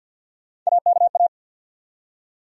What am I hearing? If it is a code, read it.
Morse code, heard as a keyed tone, about 49 wpm, 700 Hz. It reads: AKA
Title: USA Call Sign Postfixes - 3 Letters 50wpm